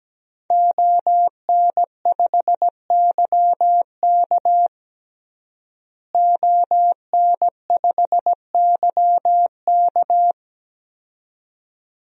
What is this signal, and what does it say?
Morse code, heard as a keyed tone, about 17 wpm, 700 Hz. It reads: ON5YK ON5YK
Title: DXCC-100 Call Signs - 5 Characters - Repeated 2x - 17wpm